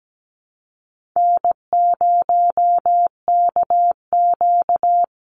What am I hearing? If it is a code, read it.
Morse code, heard as a keyed tone, about 17 wpm, 700 Hz. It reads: N0KQ
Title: Easy Call Signs Repeated 2x - 17wpm